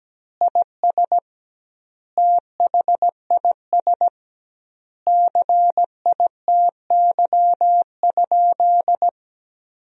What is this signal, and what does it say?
Morse code, heard as a keyed tone, about 17 wpm, 700 Hz. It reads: IS THIS CITY?
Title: 2x Word-Spacing - Sentences from Top 300 Words - 17wpm